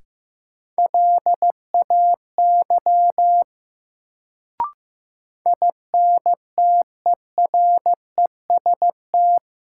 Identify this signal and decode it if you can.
Morse code, heard as a keyed tone, about 15 wpm, 700 Hz. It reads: LAY INTEREST
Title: Sentences from Top 400 Words - New Word Drill - 15wpm